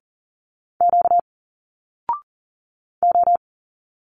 Morse code, heard as a keyed tone, about 40 wpm, 700 Hz. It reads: Q O
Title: Single Letters 40wpm